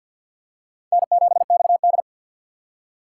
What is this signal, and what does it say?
Morse code, heard as a keyed tone, about 47 wpm, 700 Hz. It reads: N7XD
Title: USA Call Signs - 4 Characters - 2x Character Spacing - 50wpm